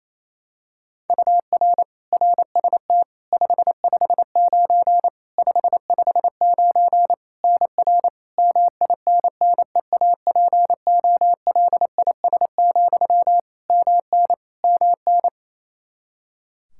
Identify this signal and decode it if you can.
Morse code, heard as a keyed tone, about 28 wpm, 700 Hz. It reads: UR RST 559 559 NR MINNEAPOLIS, MN MN
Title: Practice QSO - Exchange 1 - Beginning - 28wpm